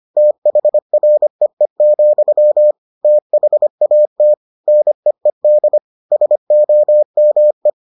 Morse code, heard as a keyed tone, about 25 wpm, 600 Hz. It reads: THREE, THAT NEED SOME